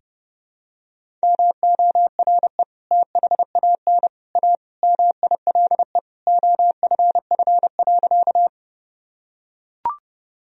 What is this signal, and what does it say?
Morse code, heard as a keyed tone, about 32 wpm, 700 Hz. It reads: MORE THAN A MILE OFF.
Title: Sentences from Top 1000 Words - Review Part 1 - 30wpm